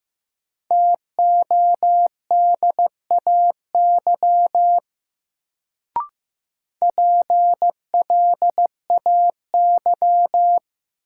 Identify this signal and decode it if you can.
Morse code, heard as a keyed tone, about 15 wpm, 700 Hz. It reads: TODAY PLAY
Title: Sets of 5 Words Encore - New Word Drill - 15wpm